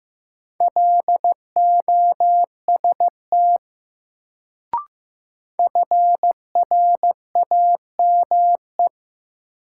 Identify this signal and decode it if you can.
Morse code, heard as a keyed tone, about 15 wpm, 700 Hz. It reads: LOST FRAME
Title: Sentences from Top 1400 Words - New Word Drill Alt - 15wpm